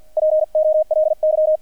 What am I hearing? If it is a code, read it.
Morse code, heard as a keyed tone, about 50 wpm, spaced as for 44 wpm, 620 Hz. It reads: POLY
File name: archived / CRSnet